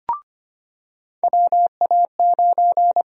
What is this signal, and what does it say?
Morse code, heard as a keyed tone, about 25 wpm, 700 Hz. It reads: WA9
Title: USA Call Sign Prefixes - 25wpm